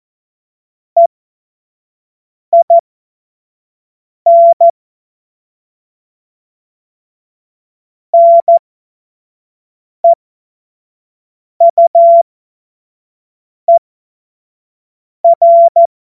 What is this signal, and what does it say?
Morse code, heard as a keyed tone, about 14 wpm, spaced as for 5 wpm, 680 Hz. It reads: EIN NEUER
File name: Morsekurs / Witze / Witz 31.Oktober 2023 25 F70